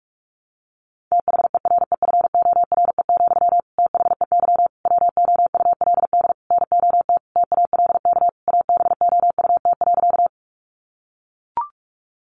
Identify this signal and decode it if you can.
Morse code, heard as a keyed tone, about 44 wpm, 700 Hz. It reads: THEREFORE, THEY WOULD NOT TALK ABOUT.
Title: Sentences from Top 800 Words 45wpm